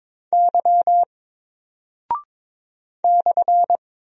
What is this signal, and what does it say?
Morse code, heard as a keyed tone, about 23 wpm, 700 Hz. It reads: Y /
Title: Single Character (Letter:Number:Punctuation:Prosign) Alt 22wpm